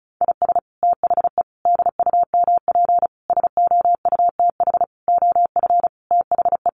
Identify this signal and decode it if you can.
Morse code, heard as a keyed tone, about 34 wpm, 700 Hz. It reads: IS THE DUMP SOUTH OF THE
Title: Sentences from Top 1300 Words 35wpm